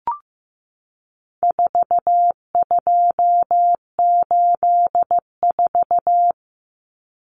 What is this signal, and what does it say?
Morse code, heard as a keyed tone, about 15 wpm, 700 Hz. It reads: 4284
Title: Common Cut Numbers - 4-Digit 15wpm